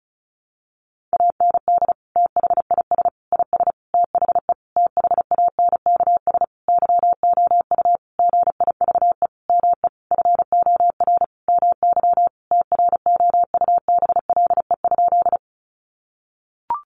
Morse code, heard as a keyed tone, about 34 wpm, 700 Hz. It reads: AND THIS IS THE THANKS YOU GIVE ME FOR MY TROUBLE?
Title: Sentences from Top 1200 Words 35wpm